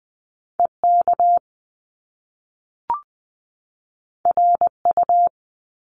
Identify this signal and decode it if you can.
Morse code, heard as a keyed tone, about 20 wpm, 700 Hz. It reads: EK RU
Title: Top Two Letter Combinations in Words 20wpm